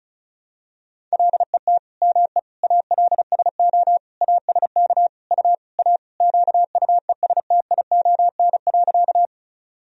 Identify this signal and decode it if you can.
Morse code, heard as a keyed tone, about 36 wpm, 700 Hz. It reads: LET ME ALSO ASK U A QUESTION.
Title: Sentences from Top 400 Words and Abbreviations 35wpm